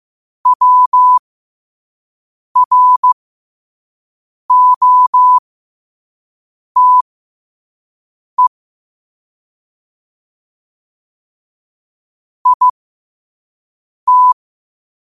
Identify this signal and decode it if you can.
Morse code, heard as a keyed tone, about 15 wpm, spaced as for 5 wpm, 1000 Hz. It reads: WROTE IT
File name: qotdfiles / 5wpm